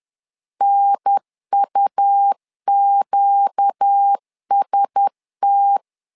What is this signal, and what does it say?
Morse code, heard as a keyed tone, about 10 wpm, 800 Hz. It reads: NUQST
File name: LEZIONI / 5 LBVC / 5 L 10wpm 2259